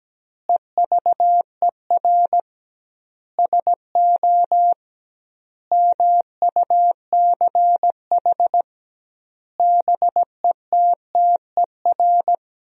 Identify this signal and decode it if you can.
Morse code, heard as a keyed tone, about 17 wpm, 700 Hz. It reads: EVER SO MUCH BETTER
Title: 2x Word-Spacing - Sentences from Top 400 Words - 17wpm